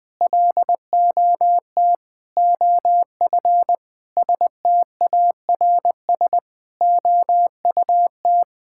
Morse code, heard as a keyed tone, about 20 wpm, 700 Hz. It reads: LOT OF STARS OUT
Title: Sentences with 5-Word Phrases - Spoken Beginning - 20wpm